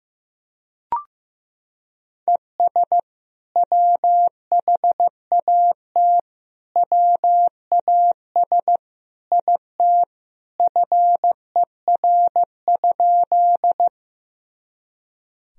Morse code, heard as a keyed tone, about 16 wpm, 700 Hz. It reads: ES WHAT WAS IT FER?
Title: Sentences from Top 100 Words and Abbreviations 15wpm